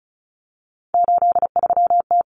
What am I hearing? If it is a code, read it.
Morse code, heard as a keyed tone, about 34 wpm, 700 Hz. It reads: 83T
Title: Common Cut Numbers - 3-Digit 35wpm